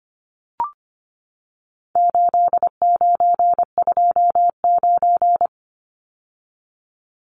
Morse code, heard as a keyed tone, about 25 wpm, 700 Hz. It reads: 8929
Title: Numbers - 4-Digit 25wpm